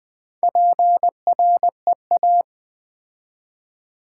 Morse code, heard as a keyed tone, about 20 wpm, 700 Hz. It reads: PREA
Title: Build A Word - 2 Letters At A Time - 6-Letter Words - 20wpm